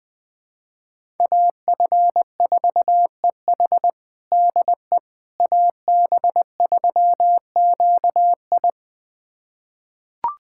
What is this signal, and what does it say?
Morse code, heard as a keyed tone, about 20 wpm, 700 Hz. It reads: AF4EH DE AB3QI
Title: USA Callsign DE Callsign - 5 Characters - 20wpm